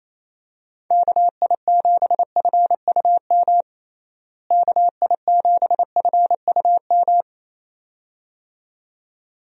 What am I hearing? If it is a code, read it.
Morse code, heard as a keyed tone, about 28 wpm, 700 Hz. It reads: KI7FUM KI7FUM
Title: USA Call Signs - 6 Characters - Repeated 2x - 28wpm